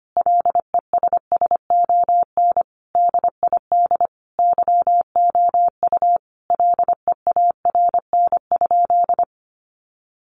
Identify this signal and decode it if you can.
Morse code, heard as a keyed tone, about 25 wpm, 700 Hz. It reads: LESSON DID YOU LEARN?
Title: Sentences from Top 1200 Words 25wpm